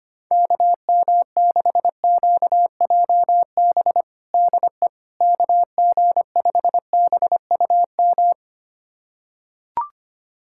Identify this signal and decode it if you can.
Morse code, heard as a keyed tone, about 25 wpm, 700 Hz. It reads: KM6QJB DE KG5BUM
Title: USA Callsign DE Callsign - 6 Characters - 25wpm